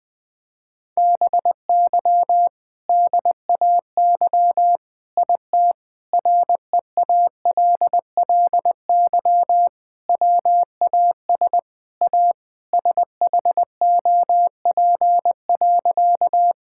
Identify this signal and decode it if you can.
Morse code, heard as a keyed tone, about 20 wpm, 700 Hz. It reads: BY DAY IT REALLY WAS A SHOP.
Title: Sentences from Top 1600 Words 20wpm